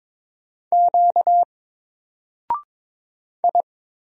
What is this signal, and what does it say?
Morse code, heard as a keyed tone, about 22 wpm, 700 Hz. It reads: Q I
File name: Single-Letter-Number-22wpm